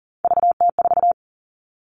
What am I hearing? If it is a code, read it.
Morse code, heard as a keyed tone, about 40 wpm, 700 Hz. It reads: VT4
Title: All Cut Numbers - 3-Digit 40wpm